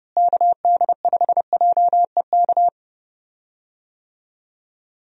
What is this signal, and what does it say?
Morse code, heard as a keyed tone, about 30 wpm, 700 Hz. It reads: KD5JEK
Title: USA Call Signs - 6 Characters - Repeated 3x - 30wpm